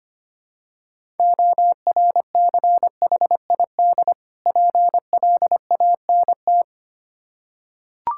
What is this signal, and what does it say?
Morse code, heard as a keyed tone, about 25 wpm, 700 Hz. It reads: ORCHID PLANT
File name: practice / 25wpm / Difficult Words with Hints - 25wpm